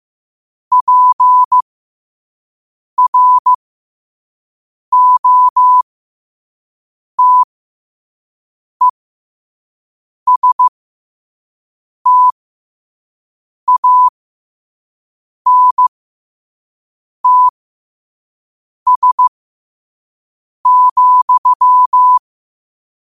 Morse code, heard as a keyed tone, about 15 wpm, spaced as for 5 wpm, 1000 Hz. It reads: PROTESTANTS,
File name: qotdfiles / 5wpm